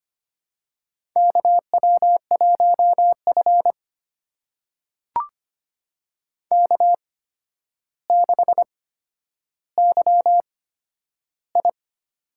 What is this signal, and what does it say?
Morse code, heard as a keyed tone, about 25 wpm, spaced as for 11 wpm, 700 Hz. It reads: KW1F K6YI
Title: USA Call Signs - 4 Characters - 8x Character Spacing - 25wpm